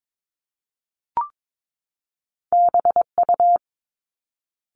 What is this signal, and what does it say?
Morse code, heard as a keyed tone, about 22 wpm, 700 Hz. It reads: BU
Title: Top Two Letter Combinations in Words 22wpm